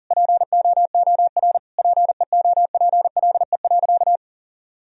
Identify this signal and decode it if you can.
Morse code, heard as a keyed tone, about 40 wpm, 700 Hz. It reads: POOR PEOPLE.
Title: Sentences from Top 1000 Words - Review Part 1 - 40wpm